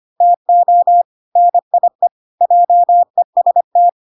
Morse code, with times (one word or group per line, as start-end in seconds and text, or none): 0.2-1.0 TO
1.4-2.1 NIE
2.4-3.9 JEST